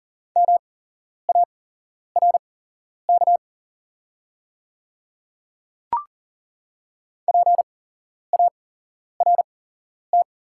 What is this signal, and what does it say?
Morse code, heard as a keyed tone, about 40 wpm, spaced as for 11 wpm, 690 Hz. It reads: MARK PART
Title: 8x Character Spacing - Top 300 Words - 40wpm